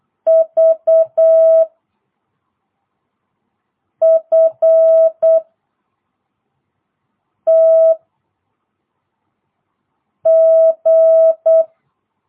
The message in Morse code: V F T G

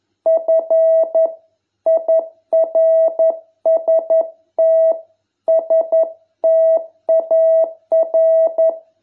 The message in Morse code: FIRST STAR

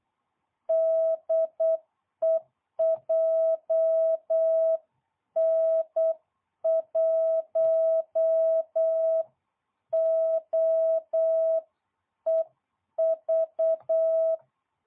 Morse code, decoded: DE JN1OEV

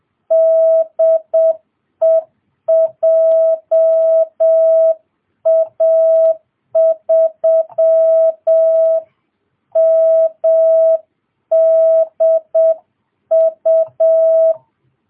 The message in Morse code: DE JA3MDU

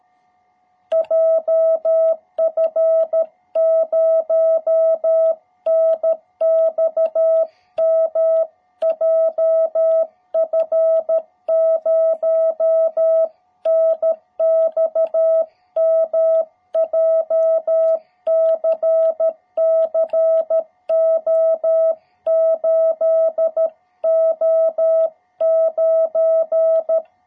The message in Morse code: JF0NXM JF0NXM JCC O8O9

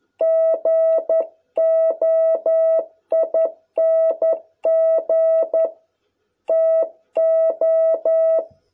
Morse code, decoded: GOING TO

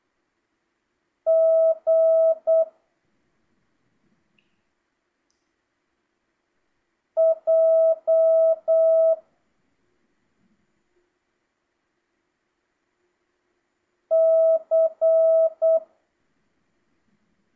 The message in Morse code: G J C

